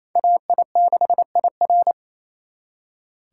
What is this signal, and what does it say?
Morse code, heard as a keyed tone, about 34 wpm, 700 Hz. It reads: AI6IR
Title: Speed Racing - USA 2x2 Call Signs 28wpm